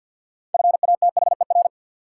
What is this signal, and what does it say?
Morse code, heard as a keyed tone, about 53 wpm, 700 Hz. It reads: FATHER
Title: Sentences from Top 300 Words - New Word Drill - 50wpm